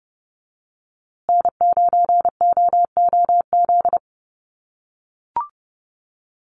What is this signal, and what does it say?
Morse code, heard as a keyed tone, about 30 wpm, 700 Hz. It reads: N9OOZ
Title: USA Call Signs Repeated 3x 30wpm